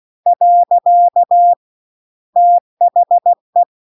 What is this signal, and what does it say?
Morse code, heard as a keyed tone, about 16 wpm, 700 Hz. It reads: . THE